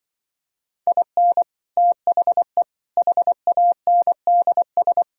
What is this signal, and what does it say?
Morse code, heard as a keyed tone, about 25 wpm, 700 Hz. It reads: IN THE HANDS